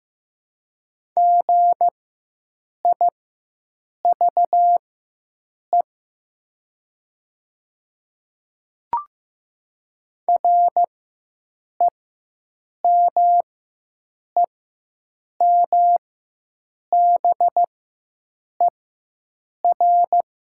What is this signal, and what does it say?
Morse code, heard as a keyed tone, about 15 wpm, spaced as for 7 wpm, 690 Hz. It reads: GIVE REMEMBER
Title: 4x Character Spacing - Top 500 Words - 15wpm